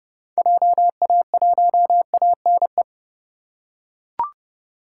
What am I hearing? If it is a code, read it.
Morse code, heard as a keyed tone, about 29 wpm, 700 Hz. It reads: JA1ANE